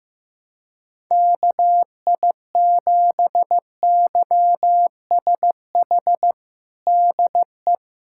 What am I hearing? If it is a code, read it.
Morse code, heard as a keyed tone, about 16 wpm, 700 Hz. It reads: KI7YSH DE
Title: USA Callsign DE Callsign 15wpm